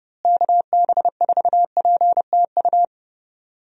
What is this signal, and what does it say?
Morse code, heard as a keyed tone, about 32 wpm, 700 Hz. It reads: KB4PTU
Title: USA Call Signs - 6 Characters - Repeated 2x - 30wpm